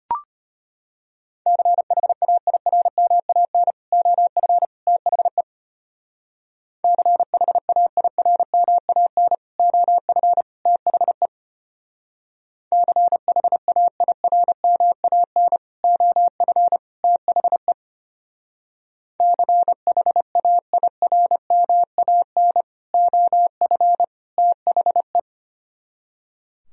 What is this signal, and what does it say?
Morse code, heard as a keyed tone, about 32 wpm, 700 Hz. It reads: CHAIRMAN OF THE CHAIRMAN OF THE CHAIRMAN OF THE CHAIRMAN OF THE